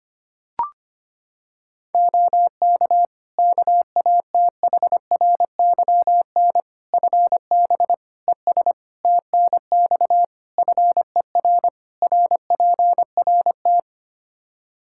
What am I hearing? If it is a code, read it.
Morse code, heard as a keyed tone, about 25 wpm, 700 Hz. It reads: OK KATHRYN FB ES TNX FER RPRT
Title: K1USN SST QSO Practice 25wpm